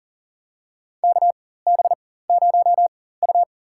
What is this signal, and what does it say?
Morse code, heard as a keyed tone, about 40 wpm, spaced as for 19 wpm, 700 Hz. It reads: KB0U
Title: USA Call Signs - 4 Characters - 4x Character Spacing - 40wpm